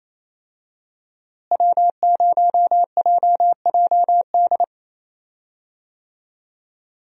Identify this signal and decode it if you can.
Morse code, heard as a keyed tone, about 28 wpm, 700 Hz. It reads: W0JJD
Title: USA Call Signs - 5 Characters - Repeated 3x - 28wpm